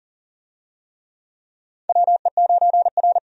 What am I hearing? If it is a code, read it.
Morse code, heard as a keyed tone, about 40 wpm, 700 Hz. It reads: WE9R
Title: USA Call Signs - 4 Characters - 40wpm